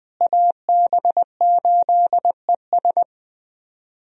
Morse code, heard as a keyed tone, about 20 wpm, 700 Hz. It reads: AB8ES
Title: USA Call Signs - 5 Characters - Repeated 2x - 20wpm